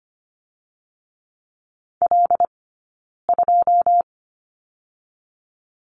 Morse code, heard as a keyed tone, about 25 wpm, spaced as for 9 wpm, 700 Hz. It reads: L2
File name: Lesson 15d - Number 5 - Two Characters 25wpm